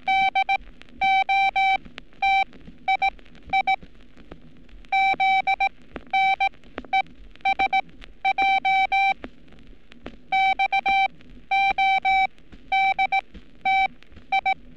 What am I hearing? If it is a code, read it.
Morse code, heard as a keyed tone, about 18 wpm, spaced as for 12 wpm, 780 Hz. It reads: DOTII ZNESJ XODTI
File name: training / NAVPERS 11224-11235 / NAVPERS 11227 RA RB